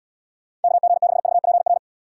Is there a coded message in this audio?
Yes: Morse code from 0.6 to 1.8 s, heard as a keyed tone, about 76 wpm, 700 Hz. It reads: KG6VPR